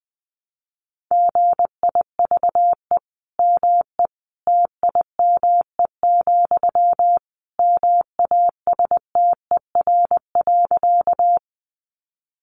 Morse code, heard as a keyed tone, about 19 wpm, 700 Hz. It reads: GIVE ME TIME, MASTER.
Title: Sentences from Top 1300 Words 20wpm